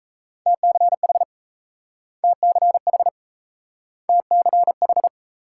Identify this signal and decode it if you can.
Morse code, heard as a keyed tone, about 38 wpm, 700 Hz. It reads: TCH TCH TCH